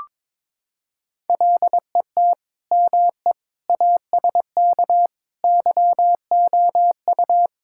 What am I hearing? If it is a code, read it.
Morse code, heard as a keyed tone, about 22 wpm, 700 Hz. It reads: LET ME ASK YOU
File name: practice / 22wpm / Sentences with 4-Word Phrases - 22wpm